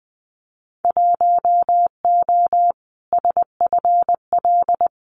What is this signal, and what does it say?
Morse code, heard as a keyed tone, about 20 wpm, 700 Hz. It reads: 1O SFL
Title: Winter Field Day Exchanges Encore 20wpm